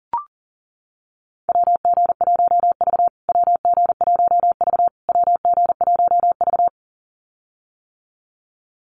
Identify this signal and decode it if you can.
Morse code, heard as a keyed tone, about 40 wpm, 700 Hz. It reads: WG1V WG1V WG1V
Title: USA Call Signs Repeated 3x 40wpm